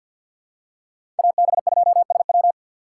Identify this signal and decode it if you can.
Morse code, heard as a keyed tone, about 52 wpm, 700 Hz. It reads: AB2SW
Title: USA Call Signs - 3x Character Spacing - 50wpm